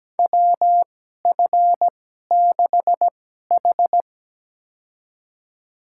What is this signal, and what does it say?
Morse code, heard as a keyed tone, about 17 wpm, spaced as for 12 wpm, 700 Hz. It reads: WF6H
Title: USA Call Signs - 4 Characters - 2x Character Spacing - 17wpm